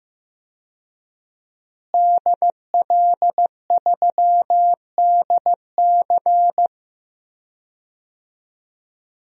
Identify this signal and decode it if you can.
Morse code, heard as a keyed tone, about 15 wpm, 700 Hz. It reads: DL3DC
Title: DXCC-100 Call Signs - 5 Characters - Repeated 3x - 15wpm